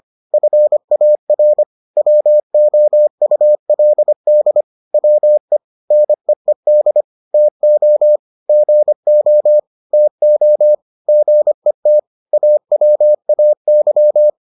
Morse code, heard as a keyed tone, about 25 wpm, 600 Hz. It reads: FAR WOULD WE NEED TO GO TO GET AWAY